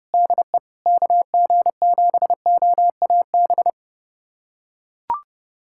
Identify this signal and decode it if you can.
Morse code, heard as a keyed tone, about 31 wpm, 700 Hz. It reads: DE KG7OAB
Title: USA Callsign DE Callsign - 6 Characters - 30wpm